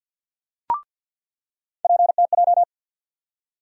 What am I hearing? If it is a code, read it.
Morse code, heard as a keyed tone, about 50 wpm, 700 Hz. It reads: PTJ
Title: USA Call Sign Postfixes - 3 Letters 50wpm